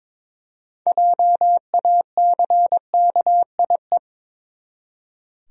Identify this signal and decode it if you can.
Morse code, heard as a keyed tone, about 22 wpm, 700 Hz. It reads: JACKIE
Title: USA Common Names - 22wpm